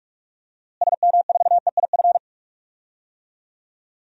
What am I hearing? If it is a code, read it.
Morse code, heard as a keyed tone, about 44 wpm, 700 Hz. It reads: SM4EIF
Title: DXCC-100 Call Signs - 6 Characters - Repeated 3x - 45wpm